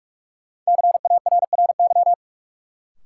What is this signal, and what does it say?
Morse code, heard as a keyed tone, about 46 wpm, 700 Hz. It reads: CARRY
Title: Sentences from Top 400 Words - New Word Drill - 45wpm